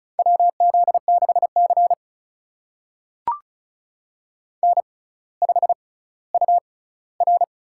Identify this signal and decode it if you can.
Morse code, heard as a keyed tone, about 36 wpm, spaced as for 18 wpm, 700 Hz. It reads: WZ6C N5UR
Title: USA Call Signs - 4 Characters - 6x Character Spacing - 35wpm